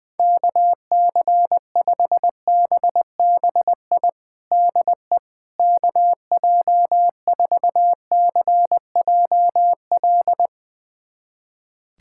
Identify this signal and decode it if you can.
Morse code, heard as a keyed tone, about 21 wpm, 700 Hz. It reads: KC5BBI DE KJ4CJL
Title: USA Callsign DE Callsign - 6 Characters - 20wpm